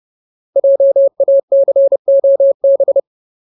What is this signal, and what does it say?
Morse code, heard as a keyed tone, about 30 wpm, 550 Hz. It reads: JACOB
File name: Ultimate Word List / 30WPM Ultimate